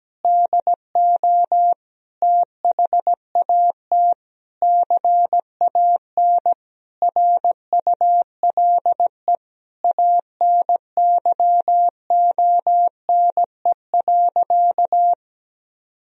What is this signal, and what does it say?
Morse code, heard as a keyed tone, about 17 wpm, 700 Hz. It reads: DO THAT CAN RULE ANYONE.
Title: Sentences from Top 1000 Words - Review Part 2 - 17wpm